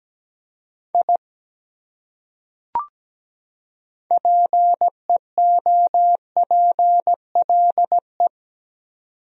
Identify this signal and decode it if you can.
Morse code, heard as a keyed tone, about 17 wpm, 700 Hz. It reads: I PEOPLE
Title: Top 100 Words 17wpm